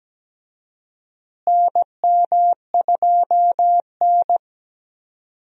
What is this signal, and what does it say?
Morse code, heard as a keyed tone, about 17 wpm, 700 Hz. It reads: NM2N
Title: USA Call Signs - 4 Characters - Repeated 3x - 17wpm